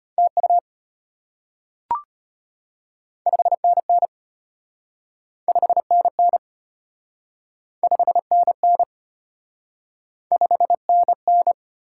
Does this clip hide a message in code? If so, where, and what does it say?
Yes: Morse code from 0.2 to 11.5 s, heard as a keyed tone, about 34 wpm, 690 Hz. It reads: TU 5NN 5NN 5NN 5NN